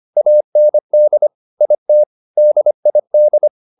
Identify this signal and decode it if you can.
Morse code, heard as a keyed tone, about 25 wpm, 600 Hz. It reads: AND IT DID
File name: the old man and the sea part 1 25wpm